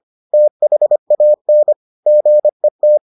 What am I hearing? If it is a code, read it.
Morse code, heard as a keyed tone, about 25 wpm, 600 Hz. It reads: THAN GET